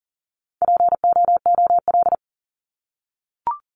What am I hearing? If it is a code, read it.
Morse code, heard as a keyed tone, about 40 wpm, 700 Hz. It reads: POOL
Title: Top 1751-2000 Words 40wpm